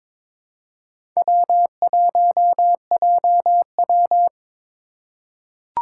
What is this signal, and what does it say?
Morse code, heard as a keyed tone, about 22 wpm, 700 Hz. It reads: W1JW
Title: USA Call Signs - 4 Characters - 4x Character Spacing - 22wpm